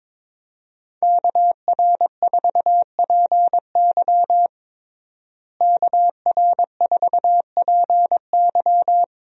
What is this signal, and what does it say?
Morse code, heard as a keyed tone, about 21 wpm, 700 Hz. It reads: KR4PY KR4PY
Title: USA Call Signs - 5 Characters - Repeated 2x - 22wpm